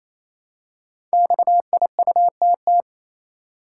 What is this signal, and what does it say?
Morse code, heard as a keyed tone, about 29 wpm, 700 Hz. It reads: XIUTT
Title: Code Groups - 5 Letters - 28wpm